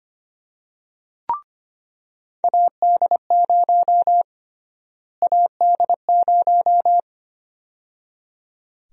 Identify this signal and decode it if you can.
Morse code, heard as a keyed tone, about 25 wpm, 700 Hz. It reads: AD0 AD0
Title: USA Call Sign Prefixes - Repeated 2x - 25wpm